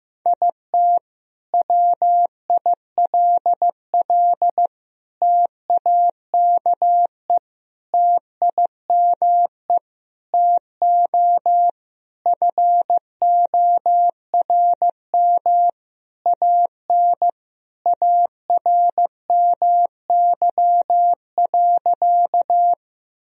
Morse code, heard as a keyed tone, about 15 wpm, 700 Hz. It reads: IT WILL TAKE TIME TO FORM AN ARMY.
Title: Sentences from Top 1000 Words - Review Part 1 - 15wpm